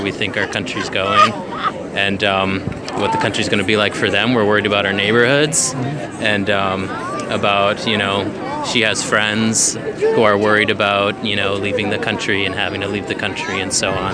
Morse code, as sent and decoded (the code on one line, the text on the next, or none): none